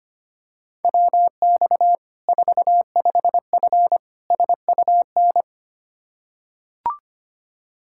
Morse code, heard as a keyed tone, about 25 wpm, 700 Hz. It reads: WX 45F SUN
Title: Practice QSO - Exchange 2 - Info - 25wpm